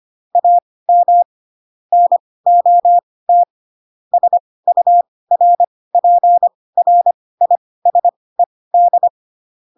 Morse code, heard as a keyed tone, about 25 wpm, spaced as for 18 wpm, 700 Hz. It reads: AM NOT SURPRISED
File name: Pinocchio ch3 12 25 ik1hge